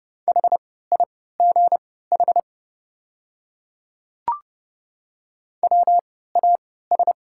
Morse code, heard as a keyed tone, about 31 wpm, spaced as for 17 wpm, 700 Hz. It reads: HIGH WAS
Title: 3x Character Spacing - Top 200 Words - 30wpm